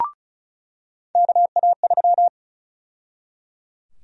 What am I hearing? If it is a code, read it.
Morse code, heard as a keyed tone, about 34 wpm, 700 Hz. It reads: KA3
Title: USA Call Sign Prefixes - 35wpm